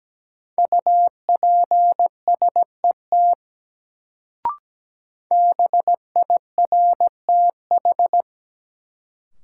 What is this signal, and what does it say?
Morse code, heard as a keyed tone, about 17 wpm, 700 Hz. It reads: UPSET BIRTH
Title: Sentences from Top 800 Words - New Word Drill - 17wpm